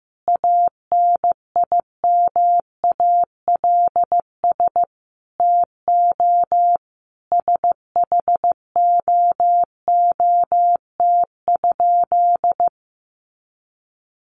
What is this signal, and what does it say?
Morse code, heard as a keyed tone, about 14 wpm, 700 Hz. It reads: ANIMALS TO SHOOT?
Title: A Sound Of Thunder 15wpm